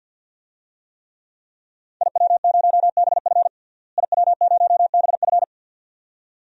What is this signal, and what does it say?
Morse code, heard as a keyed tone, about 50 wpm, 700 Hz. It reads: IW0BF IW0BF
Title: DXCC-100 Call Signs - 5 Characters - Repeated 2x - 50wpm